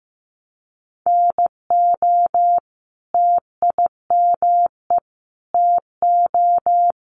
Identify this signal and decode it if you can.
Morse code, heard as a keyed tone, about 15 wpm, 700 Hz. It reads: NO TIME TO
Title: Sentences from Top 1100 Words 15wpm